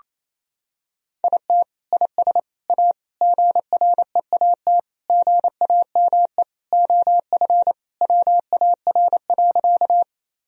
Morse code, heard as a keyed tone, about 28 wpm, 700 Hz. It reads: IT IS A GREAT GAME OF WAR.
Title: Sentences from Top 1000 Words - Review Part 3 - 28wpm